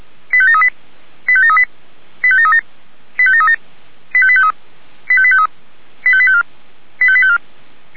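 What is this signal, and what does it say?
Five-tone selective call (ZVEI), heard as quick runs of five tones, tones of about 70 ms. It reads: 85738 85738 85737 85737 85853 85853 85854 85854
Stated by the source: ZVEI five-tone selective call 85745 received on an SDR.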